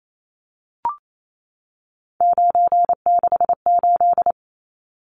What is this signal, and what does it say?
Morse code, heard as a keyed tone, about 28 wpm, 700 Hz. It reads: 968
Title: Numbers - 3-Digit 28wpm